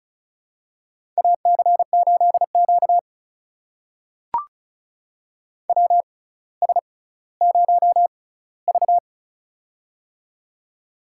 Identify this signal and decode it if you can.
Morse code, heard as a keyed tone, about 36 wpm, spaced as for 18 wpm, 700 Hz. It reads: AC8Q WS0V
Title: USA Call Signs - 4 Characters - 6x Character Spacing - 35wpm